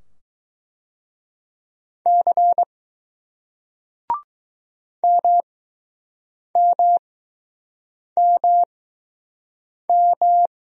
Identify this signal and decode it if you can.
Morse code, heard as a keyed tone, about 20 wpm, spaced as for 6 wpm, 700 Hz. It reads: C MMMM